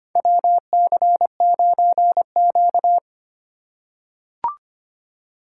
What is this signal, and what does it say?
Morse code, heard as a keyed tone, about 25 wpm, 700 Hz. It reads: WC9Q
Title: USA Call Signs - 4x Character Spacing - 25wpm